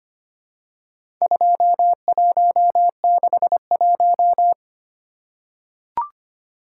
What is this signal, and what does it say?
Morse code, heard as a keyed tone, about 25 wpm, 700 Hz. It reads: 2161
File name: practice / 25wpm / Four Letter-Number Combinations - 25wpm